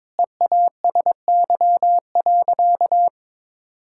Morse code, heard as a keyed tone, about 22 wpm, 700 Hz. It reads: EASY.
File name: practice / 22wpm / Sentences with 4-Word Phrases - 22wpm